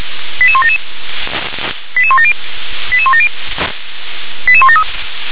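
Five-tone selective call (ZVEI), heard as quick runs of five tones, tones of about 70 ms. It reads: 88177 88170 88170 88173
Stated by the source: ZVEI five-tone call received by SDR.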